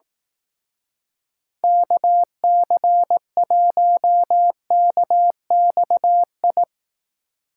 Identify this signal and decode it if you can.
Morse code, heard as a keyed tone, about 18 wpm, 700 Hz. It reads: KC1KXI
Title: Speed Racing - USA 2x3 Call Signs 15wpm